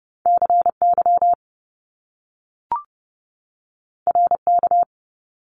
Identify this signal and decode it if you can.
Morse code, heard as a keyed tone, about 30 wpm, 700 Hz. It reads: CY RK
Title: Top Two Letter Combinations in Words 30wpm